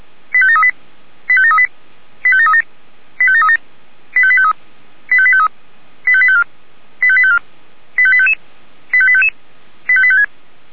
Five-tone selective call (ZVEI), heard as quick runs of five tones, tones of about 70 ms. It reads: 85738 85738 85737 85737 85853 85853 85854 85854 85855 85855 85856
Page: /464/SDR/ZVEI/